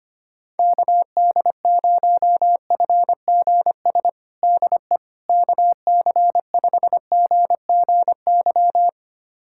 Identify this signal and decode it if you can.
Morse code, heard as a keyed tone, about 25 wpm, 700 Hz. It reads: KD0FGS DE KC5GGY
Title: USA Callsign DE Callsign - 6 Characters - 25wpm